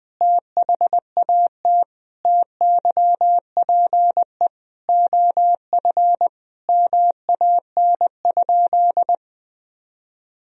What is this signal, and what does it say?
Morse code, heard as a keyed tone, about 20 wpm, 700 Hz. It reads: THAT TYPE OF MAN?
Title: Sentences from Top 1000 Words - Review Part 3 - 20wpm